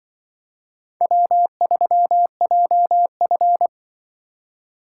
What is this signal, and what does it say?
Morse code, heard as a keyed tone, about 24 wpm, 700 Hz. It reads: W3JF
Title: Speed Racing - USA Call Signs 20wpm